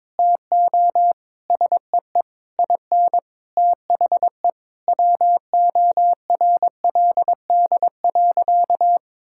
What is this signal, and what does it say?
Morse code, heard as a keyed tone, about 22 wpm, 700 Hz. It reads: TO SEE IN THE WORLD.
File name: Sentences with 9-Word Phrases - 22wpm